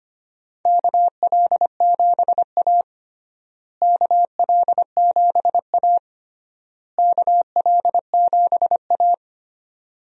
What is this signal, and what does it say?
Morse code, heard as a keyed tone, about 25 wpm, 700 Hz. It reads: KL7A KL7A KL7A
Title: DXCC-100 Call Signs - 4 Characters - Repeated 3x - 25wpm